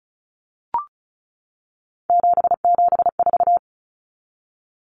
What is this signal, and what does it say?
Morse code, heard as a keyed tone, about 34 wpm, 700 Hz. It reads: 774
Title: Common Cut Numbers - 3-Digit 35wpm